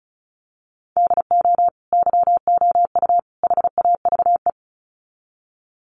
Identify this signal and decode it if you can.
Morse code, heard as a keyed tone, about 34 wpm, 700 Hz. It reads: DO YOU HAVE
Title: Sets of 3 Words 35wpm